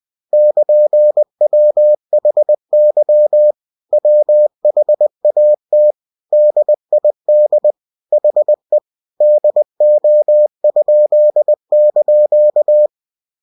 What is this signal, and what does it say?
Morse code, heard as a keyed tone, about 20 wpm, 600 Hz. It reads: <KN>WHY WHAT DID HE DO?)
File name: cw-books / winnie-the-pooh / winnie-20wpm-page-0103